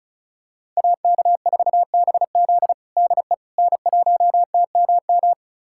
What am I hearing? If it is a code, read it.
Morse code, heard as a keyed tone, about 36 wpm, 700 Hz. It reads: AK4BZ DE N1TMM
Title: USA Callsign DE Callsign - 5 Characters - 35wpm